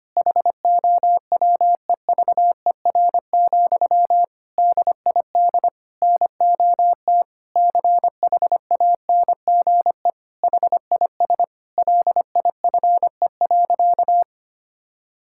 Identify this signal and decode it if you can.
Morse code, heard as a keyed tone, about 25 wpm, 700 Hz. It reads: HOWEVER, DID NOT CHANGE HIS LIFE.
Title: Sentences from Top 1000 Words - Review Part 5 - 25wpm